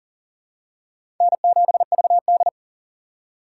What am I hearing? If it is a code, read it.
Morse code, heard as a keyed tone, about 40 wpm, 690 Hz. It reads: N7VD
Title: USA Call Signs - 4 Characters - 2x Character Spacing - 40wpm